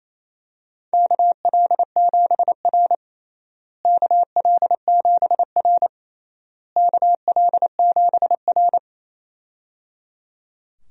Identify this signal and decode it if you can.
Morse code, heard as a keyed tone, about 28 wpm, 700 Hz. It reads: KL7R KL7R KL7R
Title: DXCC-100 Call Signs - 4 Characters - Repeated 3x - 28wpm